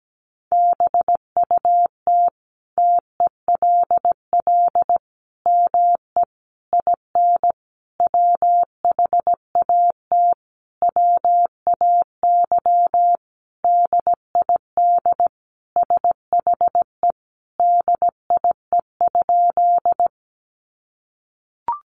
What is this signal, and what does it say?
Morse code, heard as a keyed tone, about 17 wpm, 700 Hz. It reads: BUT TELL ME IN WHAT WAY DID SHE DIE?
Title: Sentences from Top 1300 Words 17wpm